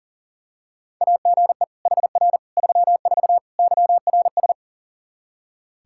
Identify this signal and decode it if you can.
Morse code, heard as a keyed tone, about 37 wpm, 700 Hz. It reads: AGE HR 34 YRS
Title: Practice QSO - Exchange 3 - More Info - 40wpm